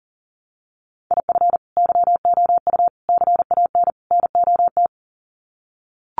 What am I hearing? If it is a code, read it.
Morse code, heard as a keyed tone, about 40 wpm, 700 Hz. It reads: IF YOU CAN NOT
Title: Sets of 4 Words 40wpm